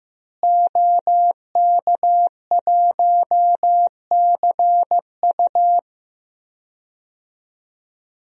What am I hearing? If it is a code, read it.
Morse code, heard as a keyed tone, about 15 wpm, 700 Hz. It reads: OK1CU
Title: DXCC-100 Call Signs - 5 Characters - Repeated 2x - 15wpm